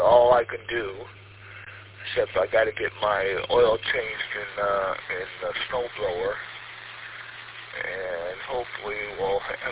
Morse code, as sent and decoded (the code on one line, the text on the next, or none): none